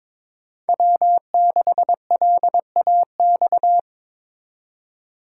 Morse code, heard as a keyed tone, about 23 wpm, 700 Hz. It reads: W6LAX
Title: K1USN SST QSO Practice - 22wpm